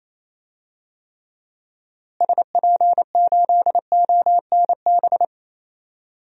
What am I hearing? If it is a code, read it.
Morse code, heard as a keyed tone, about 28 wpm, 700 Hz. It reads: SP8ONB
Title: DXCC-100 Call Signs - 6 Characters - Repeated 3x - 28wpm